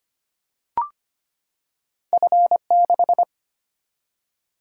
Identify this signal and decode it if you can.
Morse code, heard as a keyed tone, about 25 wpm, 700 Hz. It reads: F6
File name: Two Letter-Number Combinations - 25wpm